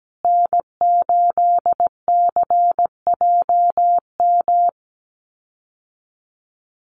Morse code, heard as a keyed tone, about 17 wpm, 700 Hz. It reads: N8CJM
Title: USA Call Signs Repeated 3x 17wpm